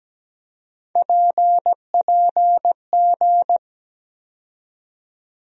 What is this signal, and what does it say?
Morse code, heard as a keyed tone, about 17 wpm, 700 Hz. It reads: PPG
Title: USA Call Sign Postfixes - 3 Letters 17wpm